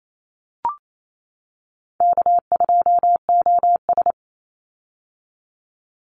Morse code, heard as a keyed tone, about 28 wpm, 700 Hz. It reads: K2OS
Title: Call Signs - 28wpm